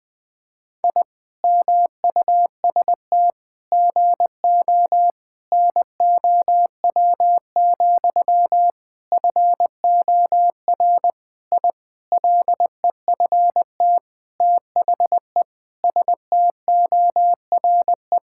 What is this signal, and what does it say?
Morse code, heard as a keyed tone, about 20 wpm, 700 Hz. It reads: I MUST GO NOW, FOR I LEFT THE STORE
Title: Sentences from Top 1000 Words - Review Part 5 - 20wpm